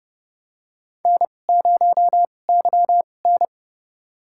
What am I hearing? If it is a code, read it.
Morse code, heard as a keyed tone, about 30 wpm, spaced as for 22 wpm, 700 Hz. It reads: N0YN
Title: USA Call Signs - 4 Characters - 2x Character Spacing - 30wpm